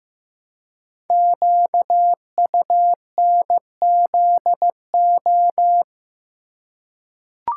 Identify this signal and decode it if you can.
Morse code, heard as a keyed tone, about 15 wpm, 700 Hz. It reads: QUNZO
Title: Code Groups - 5 Letters - 15wpm